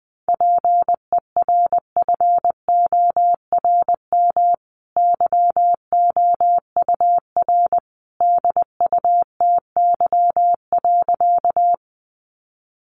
Morse code, heard as a keyed tone, about 20 wpm, 700 Hz. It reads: PERFORM YOUR DUTY.
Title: Sentences from Top 1500 Words 20wpm